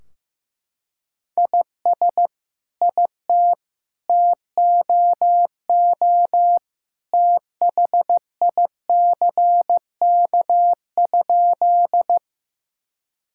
Morse code, heard as a keyed tone, about 16 wpm, 700 Hz. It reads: IS IT TOO THICK?
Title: Sentences from Top 1600 Words 15wpm